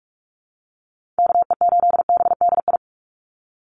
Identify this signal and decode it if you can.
Morse code, heard as a keyed tone, about 44 wpm, 700 Hz. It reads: KE8BDI
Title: USA Call Signs 45wpm